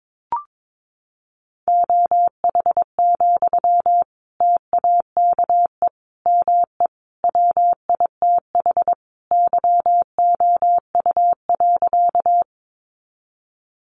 Morse code, heard as a keyed tone, about 22 wpm, 700 Hz. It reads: OH, TAKE ME WITH YOU.
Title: Sentences from Top 500 Words 22wpm